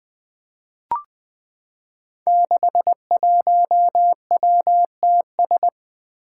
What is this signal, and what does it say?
Morse code, heard as a keyed tone, about 20 wpm, 700 Hz. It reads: 61WTS
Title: Five Challenging Letter-Number Combinations 20wpm